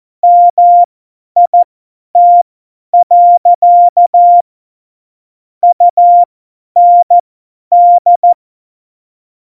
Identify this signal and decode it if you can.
Morse code, heard as a keyed tone, about 14 wpm, spaced as for 10 wpm, 700 Hz. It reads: MIT. UND
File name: Morsekurs / Witze / Witz 06. Oktober 2020, 50 F70